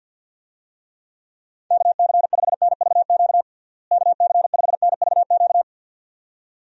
Morse code, heard as a keyed tone, about 51 wpm, 700 Hz. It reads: KC5NVQ KC5NVQ
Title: USA Call Signs - 6 Characters - Repeated 2x - 50wpm